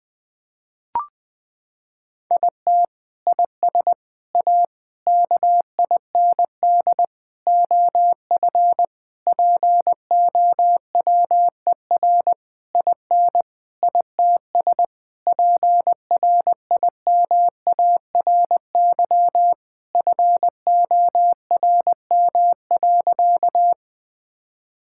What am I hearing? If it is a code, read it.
Morse code, heard as a keyed tone, about 20 wpm, 700 Hz. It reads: IT IS A KIND OF POWER IN ITS PRIMARY FORM.
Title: Sentences from Top 1000 Words - Review Part 5 - 20wpm